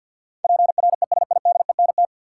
Morse code, heard as a keyed tone, about 52 wpm, 700 Hz. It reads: PRESIDENT
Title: Sentences from Top 1000 Words - New Word Drill - 50wpm